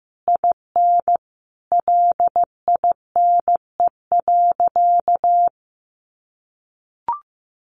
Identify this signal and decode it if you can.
Morse code, heard as a keyed tone, about 15 wpm, 700 Hz. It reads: IN LINE.
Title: Additional Easy Phrases 15wpm